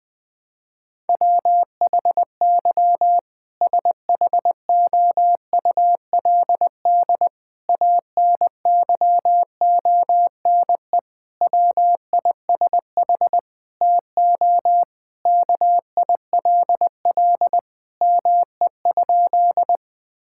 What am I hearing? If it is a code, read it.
Morse code, heard as a keyed tone, about 20 wpm, 700 Hz. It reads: WHY SHOULD ANYONE WISH TO KILL ME?
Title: Sentences from Top 1600 Words 20wpm